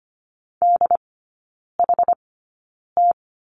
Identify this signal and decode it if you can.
Morse code, heard as a keyed tone, about 25 wpm, spaced as for 9 wpm, 700 Hz. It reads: DHT
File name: course / 25:10wpm / 093 Lesson 16e - Letter c - Three Characters 25wpm